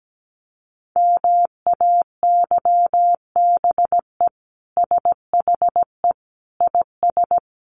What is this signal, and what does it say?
Morse code, heard as a keyed tone, about 17 wpm, 700 Hz. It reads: MAYBE SHE IS
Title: Sentences from Top 1000 Words 17wpm